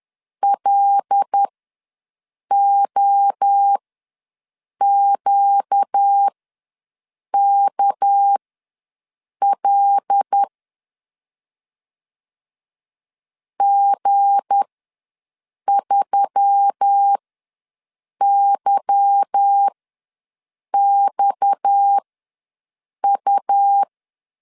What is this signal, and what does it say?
Morse code, heard as a keyed tone, about 10 wpm, spaced as for 6 wpm, 800 Hz. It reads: LOQKL G3YXU